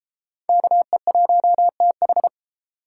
Morse code, heard as a keyed tone, about 30 wpm, 700 Hz. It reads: KE1TH